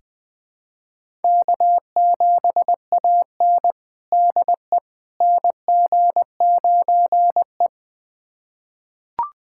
Morse code, heard as a keyed tone, about 20 wpm, 700 Hz. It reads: K7AN DE NG9E
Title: USA Callsign DE Callsign - 4 Characters - 20wpm